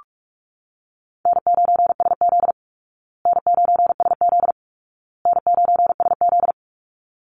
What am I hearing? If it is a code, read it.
Morse code, heard as a keyed tone, about 44 wpm, 700 Hz. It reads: N9SZ N9SZ N9SZ
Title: Easy Call Signs Repeated 3x - 45wpm